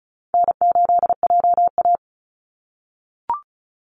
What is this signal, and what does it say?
Morse code, heard as a keyed tone, about 34 wpm, 700 Hz. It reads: N8JA
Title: Call Signs - 35wpm